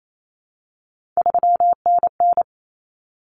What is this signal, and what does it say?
Morse code, heard as a keyed tone, about 28 wpm, 700 Hz. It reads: 3NN
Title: Common Cut Numbers - 3-Digit 28wpm